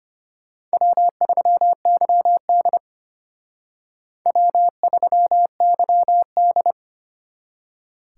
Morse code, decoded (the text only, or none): W3YD W3YD